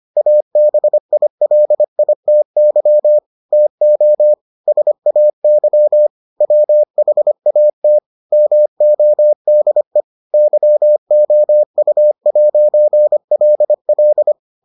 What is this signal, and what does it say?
Morse code, heard as a keyed tone, about 25 wpm, 600 Hz. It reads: ABILITY TO SAY WHAT MODE YOU'LL